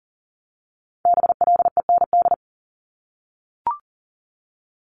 Morse code, heard as a keyed tone, about 42 wpm, 700 Hz. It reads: BLEND
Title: Multi-letter Phonemes 40wpm